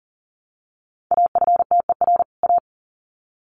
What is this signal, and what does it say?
Morse code, heard as a keyed tone, about 40 wpm, 700 Hz. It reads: AFTER A
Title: Sets of 2 Words 40wpm